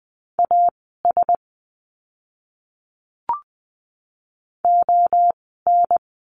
Morse code, heard as a keyed tone, about 20 wpm, spaced as for 14 wpm, 700 Hz. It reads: AS ON
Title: 036 Lesson 7f - Letter s - Words 20wpm